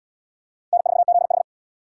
Morse code, heard as a keyed tone, about 66 wpm, 700 Hz. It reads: N4ZH